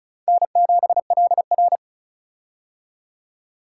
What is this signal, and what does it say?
Morse code, heard as a keyed tone, about 34 wpm, 700 Hz. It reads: N7LR
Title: K1USN SST QSO Practice - 35wpm